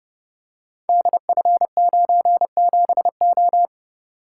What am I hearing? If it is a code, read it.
Morse code, heard as a keyed tone, about 30 wpm, 700 Hz. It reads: DF97O
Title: Code Groups - 5 Letter:Numbers - 30wpm